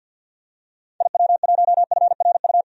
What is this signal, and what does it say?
Morse code, heard as a keyed tone, about 48 wpm, 700 Hz. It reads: IW1FRU
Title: DXCC-100 Call Signs - 6 Characters - 50wpm